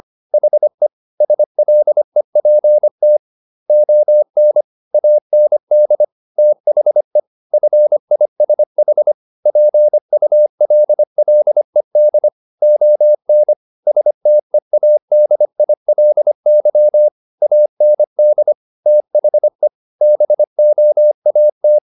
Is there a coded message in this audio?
Yes: Morse code from 0.3 to 21.8 s, heard as a keyed tone, about 25 wpm, 600 Hz. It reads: HE SLEPT ON AND THE FISH PULLED ON STEADILY AND THE BOAT